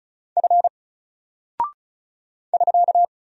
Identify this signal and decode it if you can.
Morse code, heard as a keyed tone, about 36 wpm, 700 Hz. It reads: F <SK>
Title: Single Character (Letter:Number:Punctuation:Prosign) 35wpm